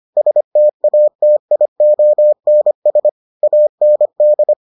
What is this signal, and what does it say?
Morse code, heard as a keyed tone, about 25 wpm, 600 Hz. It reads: STATIONS AND